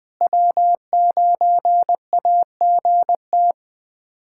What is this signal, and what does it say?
Morse code, heard as a keyed tone, about 20 wpm, 700 Hz. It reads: W9AGT
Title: USA Callsign DE Callsign - 5 Characters - 20wpm